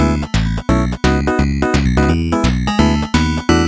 There is no code.